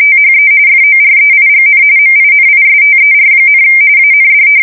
RTTY, about 45.45 baud. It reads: THE FOREIGN MINISTER, AND
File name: fsk170 / usnavy1